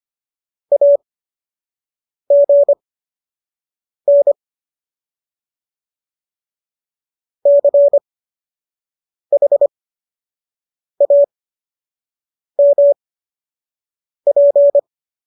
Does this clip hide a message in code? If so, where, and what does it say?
Yes: Morse code from 0.7 to 14.8 s, heard as a keyed tone, about 25 wpm, spaced as for 6 wpm, 570 Hz. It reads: AGN CHAMP